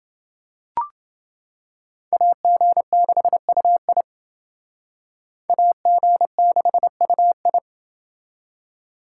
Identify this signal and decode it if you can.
Morse code, heard as a keyed tone, about 28 wpm, 700 Hz. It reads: AG6UI AG6UI